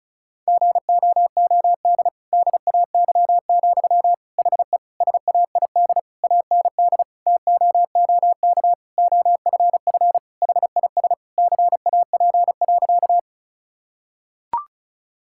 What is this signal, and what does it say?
Morse code, heard as a keyed tone, about 36 wpm, 700 Hz. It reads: GOOD DAY, HE SAID AND TOOK OFF HIS CAP.
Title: Sentences from Top 1600 Words 35wpm